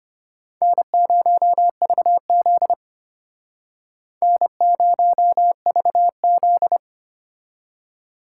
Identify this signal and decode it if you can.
Morse code, heard as a keyed tone, about 32 wpm, 700 Hz. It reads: N0VZ N0VZ